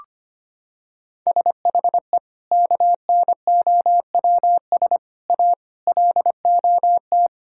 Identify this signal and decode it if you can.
Morse code, heard as a keyed tone, about 25 wpm, 700 Hz. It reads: SHE KNOWS A LOT
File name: practice / 25wpm / Sentences with 4-Word Phrases - 25wpm